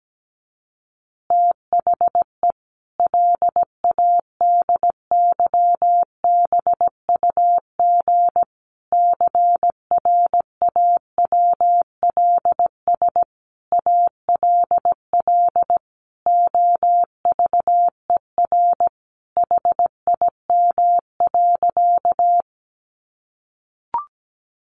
Morse code, heard as a keyed tone, about 17 wpm, 700 Hz. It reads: THE LADYBUG CRAWLS ALL OVER HIM.
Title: Additional Easy Phrases 17wpm